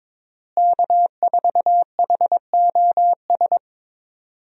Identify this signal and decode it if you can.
Morse code, heard as a keyed tone, about 21 wpm, 700 Hz. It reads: K4HOS
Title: USA Call Signs - 5 Characters - Repeated 3x - 22wpm